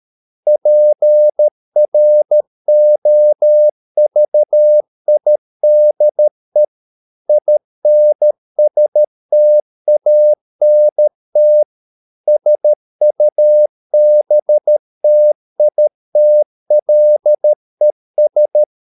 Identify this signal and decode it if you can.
Morse code, heard as a keyed tone, about 13 wpm, 600 Hz. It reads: PROVIDE INSTANT SUBTITLES